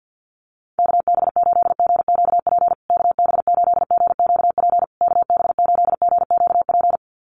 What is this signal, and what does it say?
Morse code, heard as a keyed tone, about 48 wpm, 700 Hz. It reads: KB8GQP KB8GQP KB8GQP
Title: USA Call Signs Repeated 3x 50wpm